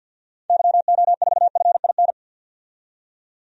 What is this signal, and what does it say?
Morse code, heard as a keyed tone, about 50 wpm, 700 Hz. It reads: YO4FIN
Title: DXCC-100 Call Signs - 6 Characters - 50wpm